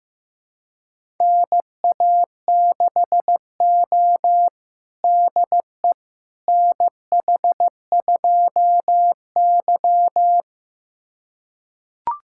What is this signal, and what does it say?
Morse code, heard as a keyed tone, about 15 wpm, 700 Hz. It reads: NA6O DE NH2Y
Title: USA Callsign DE Callsign - 4 Characters - 15wpm